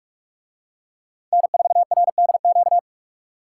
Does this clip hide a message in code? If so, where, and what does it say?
Yes: Morse code from 1.3 to 2.8 s, heard as a keyed tone, about 44 wpm, 700 Hz. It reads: N4RDQ